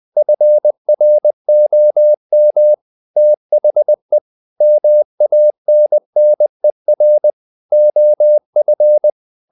Morse code, heard as a keyed tone, about 20 wpm, 600 Hz. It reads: FROM THE MANNER OF
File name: the old man and the sea part 2 20wpm